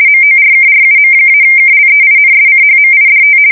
RTTY, about 50 baud. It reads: MORKRUM-KLEINSCHMIDT
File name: fsk170-50B/teletype-corp